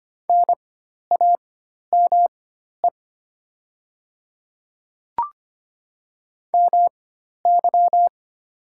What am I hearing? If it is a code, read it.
Morse code, heard as a keyed tone, about 25 wpm, spaced as for 12 wpm, 700 Hz. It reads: NAME MY
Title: 4x Character Spacing - Top 200 Words - 25wpm